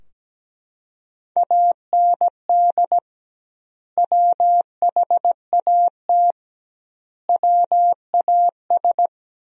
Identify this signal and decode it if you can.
Morse code, heard as a keyed tone, about 17 wpm, 700 Hz. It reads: AND WHAT WAS
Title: 2x Word-Spacing - Sentences from Top 200 Words - 17wpm